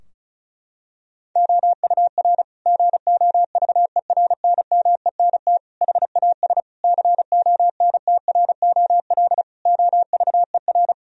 Morse code, heard as a keyed tone, about 34 wpm, 700 Hz. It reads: OUR GOVERNMENT HAS CONTROL OVER
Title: Sentences from Top 1000 Words - Review Part 4 - 35wpm